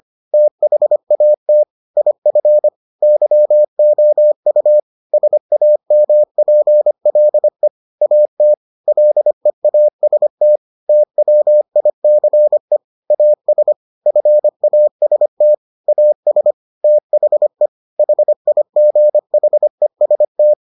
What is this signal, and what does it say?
Morse code, heard as a keyed tone, about 25 wpm, 600 Hz. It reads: THAT IF YOU SAMPLE AT LEAST TWICE AS FAST AS THE HIGHEST